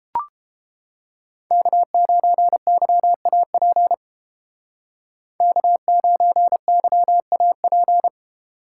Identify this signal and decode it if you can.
Morse code, heard as a keyed tone, about 30 wpm, 700 Hz. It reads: K9YAP K9YAP